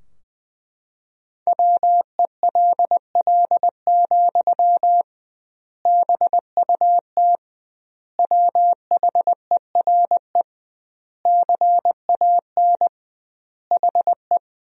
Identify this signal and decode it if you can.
Morse code, heard as a keyed tone, about 21 wpm, 700 Hz. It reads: WELL, BUT WHERE CAN HE
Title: 2x Word-Spacing - Sentences from Top 500 Words - 20wpm